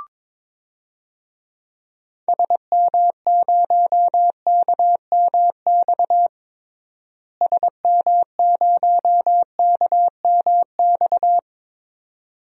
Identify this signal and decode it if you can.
Morse code, heard as a keyed tone, about 22 wpm, 700 Hz. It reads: SM0KMX SM0KMX
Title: DXCC-100 Call Signs - 6 Characters - Repeated 2x - 22wpm